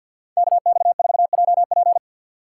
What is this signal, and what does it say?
Morse code, heard as a keyed tone, about 60 wpm, 700 Hz. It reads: KX4JP
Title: Speed Racing - USA Call Signs 50wpm